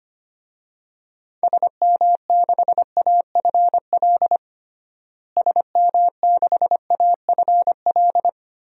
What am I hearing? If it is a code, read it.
Morse code, heard as a keyed tone, about 25 wpm, 700 Hz. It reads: SM6AFL SM6AFL
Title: DXCC-100 Call Signs - 6 Characters - Repeated 3x - 25wpm